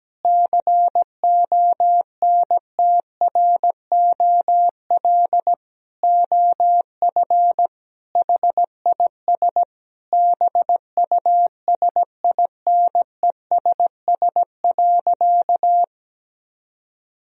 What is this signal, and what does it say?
Morse code, heard as a keyed tone, about 17 wpm, 700 Hz. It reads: CONTROL OF HIS BUSINESS.
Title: Sentences from Top 1000 Words - Review Part 3 - 17wpm